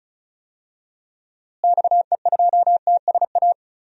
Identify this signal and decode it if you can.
Morse code, heard as a keyed tone, about 36 wpm, 700 Hz. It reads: XE2TSA
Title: DXCC-100 Call Signs - 6 Characters - Repeated 2x - 35wpm